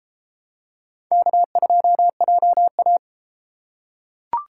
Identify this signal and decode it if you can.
Morse code, heard as a keyed tone, about 30 wpm, 700 Hz. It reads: K2JA